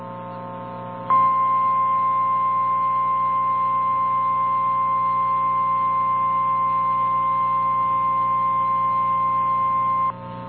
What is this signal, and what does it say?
Emergency Alert System (SAME) tones (the 1050 Hz attention tone), with noise. It is heard from 1.1 to 10.1 s.
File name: eas / KNCYKBIE / KNCYKBIE